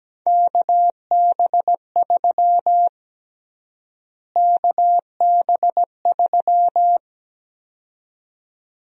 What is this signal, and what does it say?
Morse code, heard as a keyed tone, about 17 wpm, 700 Hz. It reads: KB3 KB3
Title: USA Call Sign Prefixes - Repeated 2x - 17wpm